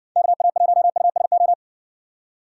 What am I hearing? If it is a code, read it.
Morse code, heard as a keyed tone, about 60 wpm, 700 Hz. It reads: KA1USO